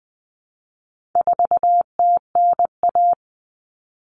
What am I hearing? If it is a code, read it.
Morse code, heard as a keyed tone, about 20 wpm, 700 Hz. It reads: 4TNA
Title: Common Cut Numbers - 4-Digit 20wpm